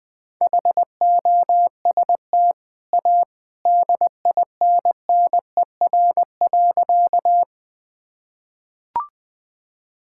Morse code, heard as a keyed tone, about 21 wpm, 700 Hz. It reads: HOST A DINNER.
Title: Easy Sentences with Hints - 20wpm